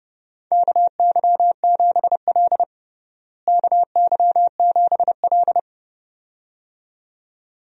Morse code, heard as a keyed tone, about 30 wpm, 700 Hz. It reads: KY7L KY7L
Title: USA Call Signs - 4 Characters - Repeated 2x - 30wpm